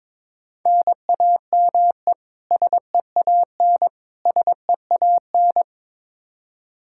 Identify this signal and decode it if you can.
Morse code, heard as a keyed tone, about 22 wpm, 700 Hz. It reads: NAME SEAN SEAN
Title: Practice QSO - Exchange 1 - Beginning - 22wpm